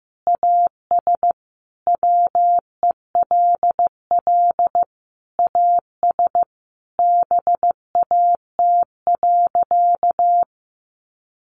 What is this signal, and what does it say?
Morse code, heard as a keyed tone, about 15 wpm, 700 Hz. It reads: AS WELL AS BAT.
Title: Sentences from Top 1500 Words 15wpm